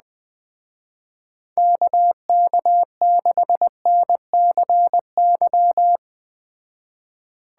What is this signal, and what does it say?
Morse code, heard as a keyed tone, about 20 wpm, 700 Hz. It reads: KK6NCY
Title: Speed Racing - USA 2x3 Call Signs 17wpm